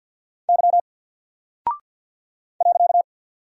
Morse code, heard as a keyed tone, about 48 wpm, 700 Hz. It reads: Y .
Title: Single Character (Letter:Number:Punctuation:Prosign) Alt 50wpm